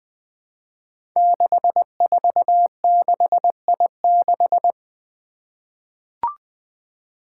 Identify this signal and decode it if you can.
Morse code, heard as a keyed tone, about 20 wpm, 700 Hz. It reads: 646I6
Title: Five Challenging Letter-Number Combinations 20wpm